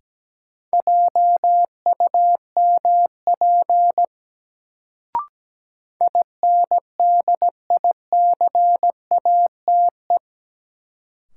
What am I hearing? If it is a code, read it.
Morse code, heard as a keyed tone, about 17 wpm, 700 Hz. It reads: JUMP INDICATE
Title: Sentences from Top 1300 Words - New Word Drill - 17wpm